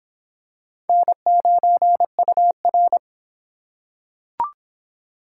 Morse code, heard as a keyed tone, about 22 wpm, 700 Hz. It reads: N9UR